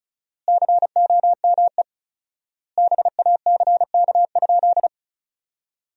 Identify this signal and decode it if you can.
Morse code, heard as a keyed tone, about 34 wpm, 700 Hz. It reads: COME BACK?
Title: 4x Word-Spacing - Sentences from Top 200 Words - 35wpm